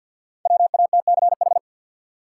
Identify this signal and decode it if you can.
Morse code, heard as a keyed tone, about 53 wpm, 700 Hz. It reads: WATCH
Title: Sentences from Top 300 Words - New Word Drill - 50wpm